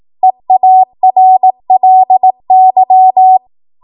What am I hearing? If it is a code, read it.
Morse code, heard as a keyed tone, about 18 wpm, 750 Hz. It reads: EARLY
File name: Archive / 18 WPM / 18WPM